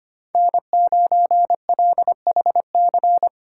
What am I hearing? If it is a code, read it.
Morse code, heard as a keyed tone, about 25 wpm, 700 Hz. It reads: N9LHC
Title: USA Call Signs - 3x Character Spacing - 25wpm